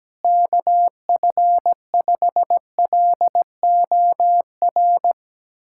Morse code, heard as a keyed tone, about 17 wpm, 700 Hz. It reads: KF5LOR
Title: USA Callsign DE Callsign 17wpm